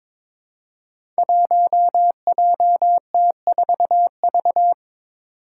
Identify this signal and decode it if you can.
Morse code, heard as a keyed tone, about 21 wpm, 700 Hz. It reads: 1JT4V
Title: Five Challenging Letter-Number Combinations 22wpm